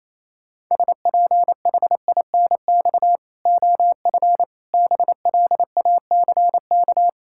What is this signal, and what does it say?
Morse code, heard as a keyed tone, about 28 wpm, 700 Hz. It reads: SPHINX OF BLACK
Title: Pangram Challenge v2 - 28wpm